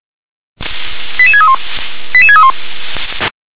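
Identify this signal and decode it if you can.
Five-tone selective call (ZVEI), heard as quick runs of five tones, tones of about 70 ms. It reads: 88631 88631
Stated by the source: ZVEI five-tone selective call received by SDR.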